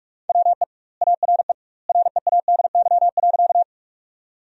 Morse code, heard as a keyed tone, about 41 wpm, 700 Hz. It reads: WE ARE READY.
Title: 2x Word-Spacing - Sentences from Top 400 Words - 45wpm